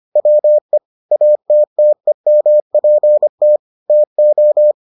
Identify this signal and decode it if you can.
Morse code, heard as a keyed tone, about 25 wpm, 600 Hz. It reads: WE ATTEMPT TO